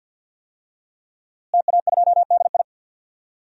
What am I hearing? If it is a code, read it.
Morse code, heard as a keyed tone, about 46 wpm, 700 Hz. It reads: TA2DI
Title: DXCC-100 Call Signs - 5 Characters - Repeated 2x - 50wpm